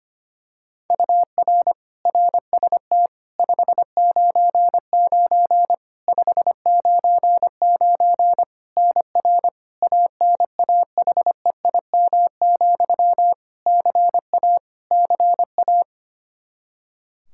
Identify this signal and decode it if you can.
Morse code, heard as a keyed tone, about 25 wpm, 700 Hz. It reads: UR RST 599 599 NR ANAHEIM, CA CA
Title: Practice QSO - Exchange 1 - Beginning - 25wpm